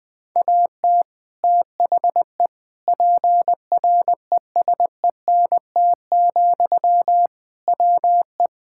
Morse code, heard as a keyed tone, about 21 wpm, 700 Hz. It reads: AT THE PRESENT, WE
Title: Sentences from Top 1000 Words - Review Part 5 - 20wpm